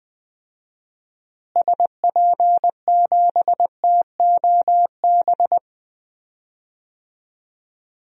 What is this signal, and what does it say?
Morse code, heard as a keyed tone, about 21 wpm, 700 Hz. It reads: SP7TOB
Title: DXCC-100 Call Signs - 6 Characters - Repeated 3x - 20wpm